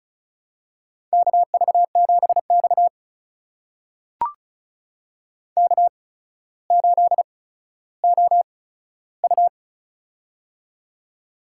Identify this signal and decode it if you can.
Morse code, heard as a keyed tone, about 34 wpm, spaced as for 15 wpm, 700 Hz. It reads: KV7X K8OU
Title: USA Call Signs - 8x Character Spacing - 35wpm